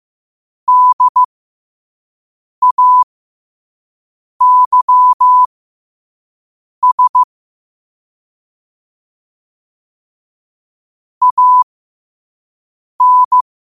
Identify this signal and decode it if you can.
Morse code, heard as a keyed tone, about 15 wpm, spaced as for 5 wpm, 1000 Hz. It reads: DAYS AN
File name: 5wpm